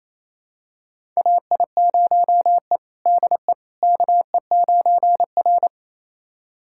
Morse code, heard as a keyed tone, about 28 wpm, 700 Hz. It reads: AI0E DE KE9R
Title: USA Callsign DE Callsign - 4 Characters - 28wpm